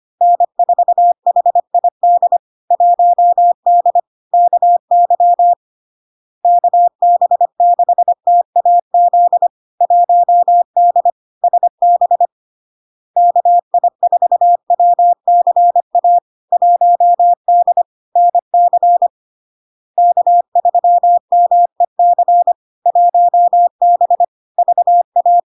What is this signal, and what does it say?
Morse code, heard as a keyed tone, about 25 wpm, 700 Hz. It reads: N4HID 1D KY KB6TAZ 1D SB KI4WCA 1D NC K3MEC 1B VA
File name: LARK-FD-2022-Log-in-CW-25wpm-700-Hz